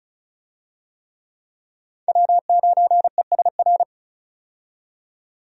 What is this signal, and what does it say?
Morse code, heard as a keyed tone, about 36 wpm, 700 Hz. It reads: W9ESR
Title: USA Call Signs - 5 Characters - 35wpm